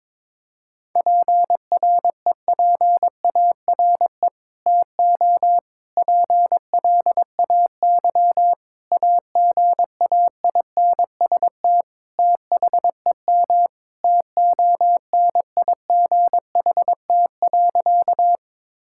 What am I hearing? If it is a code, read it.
Morse code, heard as a keyed tone, about 22 wpm, 700 Hz. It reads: PREPARE TO PLAY AGAINST THEM TONIGHT.
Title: Sentences from Top 1000 Words - Review Part 6 - 22wpm